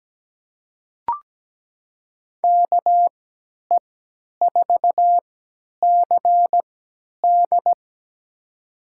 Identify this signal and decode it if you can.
Morse code, heard as a keyed tone, about 17 wpm, spaced as for 10 wpm, 700 Hz. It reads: KE4CD
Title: USA Call Signs - 5 Characters - 3x Character Spacing - 17wpm